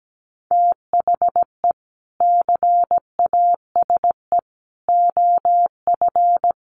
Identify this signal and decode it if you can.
Morse code, heard as a keyed tone, about 17 wpm, 700 Hz. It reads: THE CASE OF
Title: Sets of 3 Words 17wpm